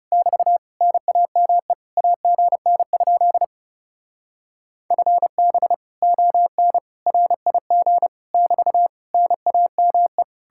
Morse code, heard as a keyed tone, about 39 wpm, 700 Hz. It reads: <BT> NAME AGN? FB ON RIG <BT> NAME